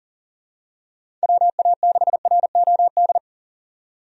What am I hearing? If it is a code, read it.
Morse code, heard as a keyed tone, about 37 wpm, 700 Hz. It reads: WA6ROD
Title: USA Call Signs - 6 Characters - 40wpm